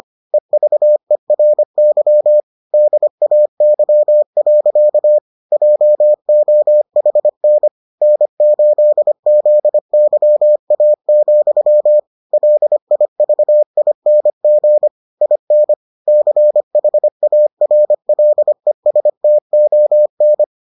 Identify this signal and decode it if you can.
Morse code, heard as a keyed tone, about 25 wpm, 600 Hz. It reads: EVERY DAY. JOHN N8ZYA, LIVING IN CHARLESTON